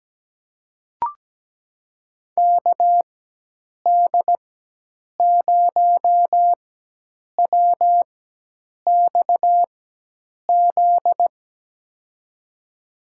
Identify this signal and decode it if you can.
Morse code, heard as a keyed tone, about 17 wpm, spaced as for 8 wpm, 700 Hz. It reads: KD0WXZ
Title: USA Call Signs - 6 Characters - 4x Character Spacing - 17wpm